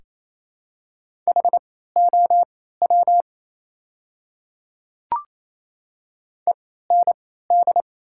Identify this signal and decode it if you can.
Morse code, heard as a keyed tone, about 27 wpm, spaced as for 16 wpm, 700 Hz. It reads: HOW END
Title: 3x Character Spacing - Top 200 Words - 28wpm